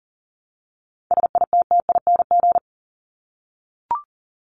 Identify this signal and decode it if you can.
Morse code, heard as a keyed tone, about 40 wpm, 700 Hz. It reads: SITTING
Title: ING Suffix - Top 100 Words with Base Words 40wpm